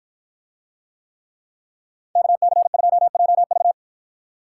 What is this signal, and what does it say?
Morse code, heard as a keyed tone, about 53 wpm, 700 Hz. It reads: KC2JV